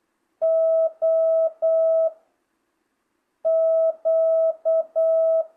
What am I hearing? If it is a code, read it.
Morse code, heard as a keyed tone, about 8 wpm, 660 Hz. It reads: O Q